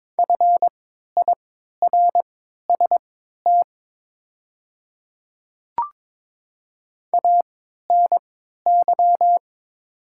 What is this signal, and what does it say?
Morse code, heard as a keyed tone, about 23 wpm, spaced as for 13 wpm, 700 Hz. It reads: FIRST ANY
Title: 3x Character Spacing - Top 200 Words - 22wpm